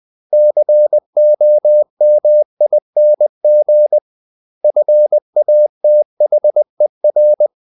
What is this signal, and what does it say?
Morse code, heard as a keyed tone, about 20 wpm, 600 Hz. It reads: COMING FATHER
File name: CW / Practice / 500MostCommonEnglishWords20